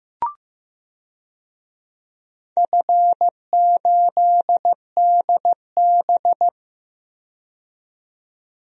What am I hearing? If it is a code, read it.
Morse code, heard as a keyed tone, about 15 wpm, 700 Hz. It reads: F8DB
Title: DXCC-100 Call Signs - 4 Characters - 15wpm